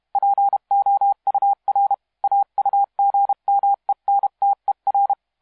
Morse code, heard as a keyed tone, about 32 wpm, 800 Hz. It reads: POUR AUGMENTER
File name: musiqueradio / cw hsc demo f6ddr